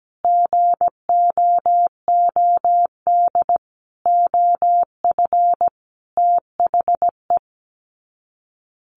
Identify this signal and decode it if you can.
Morse code, heard as a keyed tone, about 17 wpm, 700 Hz. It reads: GOOD OF THE
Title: Sets of 5 Words Encore2 17wpm